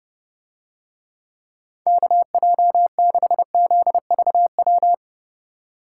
Morse code, heard as a keyed tone, about 29 wpm, 700 Hz. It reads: KJ6ZVW